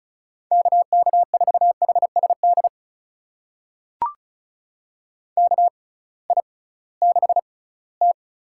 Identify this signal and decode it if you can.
Morse code, heard as a keyed tone, about 36 wpm, spaced as for 20 wpm, 700 Hz. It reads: KK4HSD KI6T
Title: USA Call Signs - 6 Characters - 6x Character Spacing - 35wpm